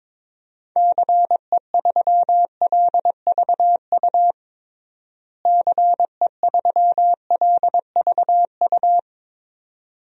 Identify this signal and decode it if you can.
Morse code, heard as a keyed tone, about 22 wpm, 700 Hz. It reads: CE3LVU CE3LVU
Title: DXCC-100 Call Signs - 6 Characters - Repeated 2x - 22wpm